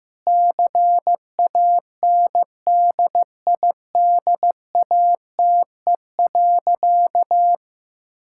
Morse code, heard as a keyed tone, about 16 wpm, 700 Hz. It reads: CANDIDATE.
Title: Sentences from Top 1600 Words 15wpm